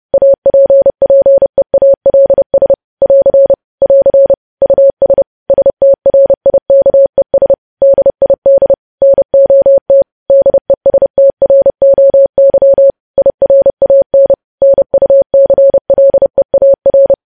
Morse code, heard as a keyed tone, about 30 wpm, 550 Hz. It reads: APPEALS <AR> <AR> US STRIKES DID NOT DESTROY IRAN NUCLEAR
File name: podcast / News headlines-1x-30x30